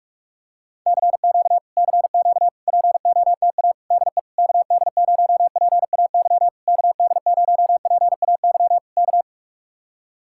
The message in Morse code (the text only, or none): CQ CQ POTA DE KD0PAY KD0PAY K